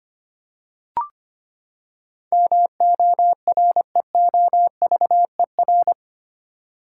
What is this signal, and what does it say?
Morse code, heard as a keyed tone, about 25 wpm, 700 Hz. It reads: MOREOVER
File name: Eight Letter Words - 25wpm